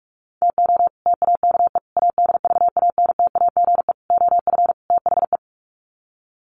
Morse code, heard as a keyed tone, about 44 wpm, 700 Hz. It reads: TO TAKE ADVANTAGE OF THE
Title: Sets of 5 Words 45wpm